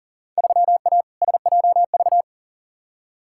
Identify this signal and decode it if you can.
Morse code, heard as a keyed tone, about 40 wpm, 700 Hz. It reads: 3A SJV
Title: Field Day Exchanges Encore - 40wpm